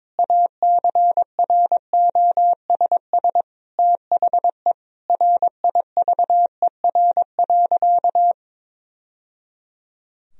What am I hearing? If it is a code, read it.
Morse code, heard as a keyed tone, about 22 wpm, 700 Hz. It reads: ACROSS THE RIVER.
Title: Sentences from Top 1000 Words - Review Part 2 - 22wpm